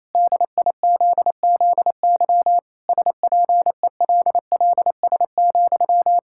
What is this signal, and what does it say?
Morse code, heard as a keyed tone, about 28 wpm, 700 Hz. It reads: DIZZY SPELLS,
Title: Pangram Challenge v2 - 28wpm